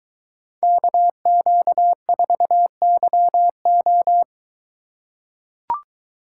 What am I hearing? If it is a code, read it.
Morse code, heard as a keyed tone, about 22 wpm, 700 Hz. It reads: KQ4YO